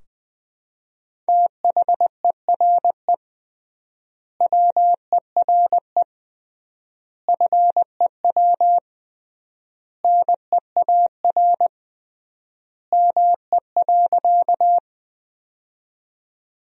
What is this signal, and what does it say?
Morse code, heard as a keyed tone, about 20 wpm, 700 Hz. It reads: THERE WERE FEW NEAR ME.
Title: 3x Word-Spacing - Sentences from Top 300 Words - 20wpm